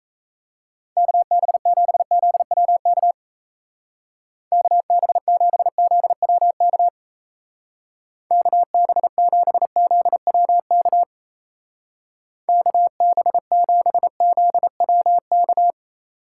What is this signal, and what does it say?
Morse code, heard as a keyed tone, about 34 wpm, 700 Hz. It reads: KB7ZWK KB7ZWK KB7ZWK KB7ZWK
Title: Speed Racing - USA 2x3 Call Signs 28wpm